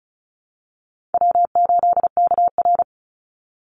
Morse code, heard as a keyed tone, about 34 wpm, 700 Hz. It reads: W8KR
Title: Call Signs Repeated 3x - 35wpm